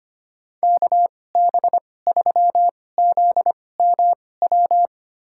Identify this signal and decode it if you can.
Morse code, heard as a keyed tone, about 25 wpm, spaced as for 18 wpm, 700 Hz. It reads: KB3ZMW
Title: USA Call Signs - 6 Characters - 2x Character Spacing - 25wpm